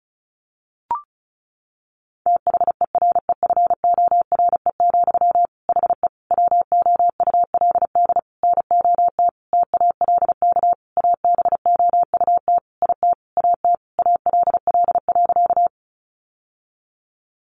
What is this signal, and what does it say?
Morse code, heard as a keyed tone, about 34 wpm, 700 Hz. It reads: THEREFORE, HE WOULD NOT TALK ABOUT IT AT ALL.
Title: Sentences from Top 800 Words 35wpm